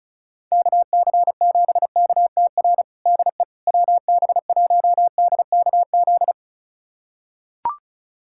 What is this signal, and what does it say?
Morse code, heard as a keyed tone, about 36 wpm, 700 Hz. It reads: KC7KTR DE WB1DKZ
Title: USA Callsign DE Callsign 35wpm